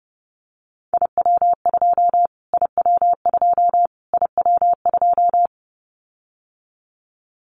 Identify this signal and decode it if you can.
Morse code, heard as a keyed tone, about 30 wpm, 700 Hz. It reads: IW2 IW2 IW2
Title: Callsign Prefixes for Top 25 DXCC - Repeated 3x - 30wpm